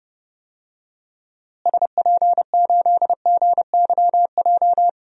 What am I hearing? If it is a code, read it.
Morse code, heard as a keyed tone, about 32 wpm, 700 Hz. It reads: SP8GYJ
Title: DXCC-100 Call Signs - 6 Characters - Repeated 2x - 30wpm